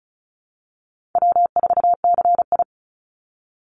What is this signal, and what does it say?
Morse code, heard as a keyed tone, about 34 wpm, 700 Hz. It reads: W4CI
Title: Call Signs Repeated 3x - 35wpm